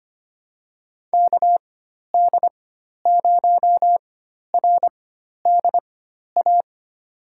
Morse code, heard as a keyed tone, about 25 wpm, spaced as for 12 wpm, 700 Hz. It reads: KD0RDA
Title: USA Call Signs - 6 Characters - 4x Character Spacing - 25wpm